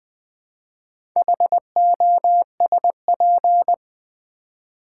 Morse code, heard as a keyed tone, about 20 wpm, 700 Hz. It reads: HOSP
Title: Build A Word - 2 Letters At A Time - 8-Letter Words - 20wpm